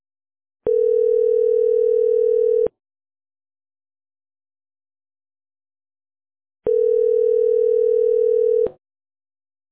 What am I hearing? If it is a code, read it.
Ringback tone (North America), 440 + 480 Hz.